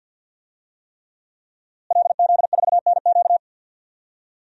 Morse code, heard as a keyed tone, about 51 wpm, 700 Hz. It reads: LZ4NQ
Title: DXCC-100 Call Signs - 5 Characters - Repeated 3x - 50wpm